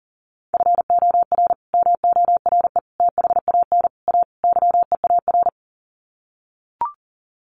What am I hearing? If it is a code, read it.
Morse code, heard as a keyed tone, about 40 wpm, 700 Hz. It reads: FOR MORE THAN A YEAR
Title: Sets of 5 Words 40wpm